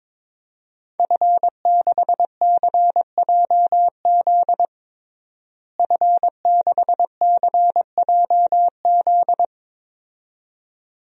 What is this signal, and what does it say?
Morse code, heard as a keyed tone, about 22 wpm, 700 Hz. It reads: F6CJZ F6CJZ
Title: DXCC-100 Call Signs - 5 Characters - Repeated 2x - 22wpm